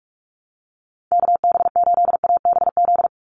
Morse code, heard as a keyed tone, about 44 wpm, 700 Hz. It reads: KB8ABZ
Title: USA Call Signs 45wpm